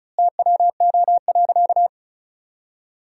Morse code, heard as a keyed tone, about 36 wpm, 700 Hz. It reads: TWO.
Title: 3x Word-Spacing - Sentences from Top 200 Words - 35wpm